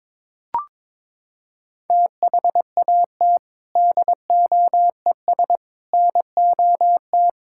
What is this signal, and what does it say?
Morse code, heard as a keyed tone, about 22 wpm, 700 Hz. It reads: THAT DOES NOT
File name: practice / 22wpm / Sentences with 4-Word Phrases - 22wpm